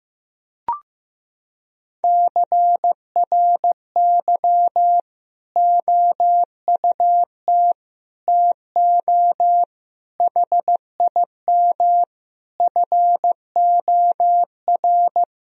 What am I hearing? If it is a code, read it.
Morse code, heard as a keyed tone, about 15 wpm, 700 Hz. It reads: CRY OUT TO HIM FOR
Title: Sentences from Top 1000 Words - Review Part 2 - 15wpm